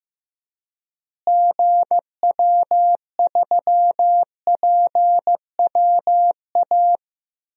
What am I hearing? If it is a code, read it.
Morse code, heard as a keyed tone, about 15 wpm, 700 Hz. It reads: GW3PWA
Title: DXCC-100 Call Signs - 6 Characters - Repeated 3x - 15wpm